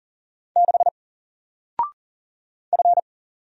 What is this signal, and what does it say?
Morse code, heard as a keyed tone, about 39 wpm, 700 Hz. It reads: 6 F
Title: Single Letter-Number Alt 40wpm